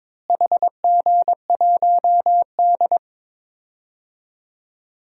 Morse code, heard as a keyed tone, about 22 wpm, 700 Hz. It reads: HG1D
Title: DXCC-100 Call Signs - 4 Characters - Repeated 2x - 22wpm